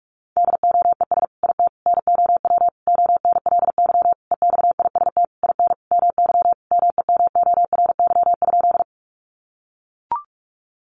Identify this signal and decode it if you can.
Morse code, heard as a keyed tone, about 44 wpm, 700 Hz. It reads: DOES IT NOW ONLY EXIST IN MY MEMORY?
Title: Sentences from Top 1300 Words 45wpm